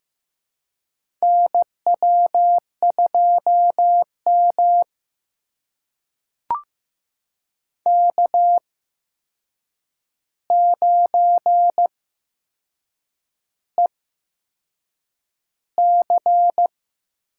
Morse code, heard as a keyed tone, about 15 wpm, spaced as for 6 wpm, 700 Hz. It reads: NW2M K9EC
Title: USA Call Signs - 4 Characters - 8x Character Spacing - 15wpm